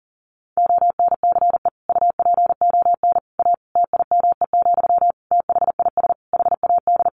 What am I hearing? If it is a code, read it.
Morse code, heard as a keyed tone, about 40 wpm, 700 Hz. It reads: ONCE UPON A TIME, THIS HAD
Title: Sentences from Top 1500 Words 40wpm